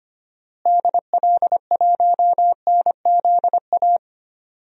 Morse code, heard as a keyed tone, about 25 wpm, 700 Hz. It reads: DL1NZA
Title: DXCC-100 Call Signs - 6 Characters - Repeated 3x - 25wpm